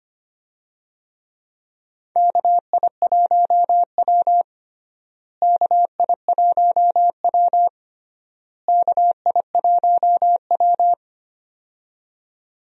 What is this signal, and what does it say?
Morse code, heard as a keyed tone, about 25 wpm, 700 Hz. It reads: KI1W KI1W KI1W
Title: USA Call Signs - 4 Characters - Repeated 3x - 25wpm